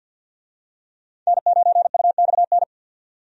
Morse code, heard as a keyed tone, about 53 wpm, 700 Hz. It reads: N9UXN
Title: USA Call Signs - 5 Characters - 50wpm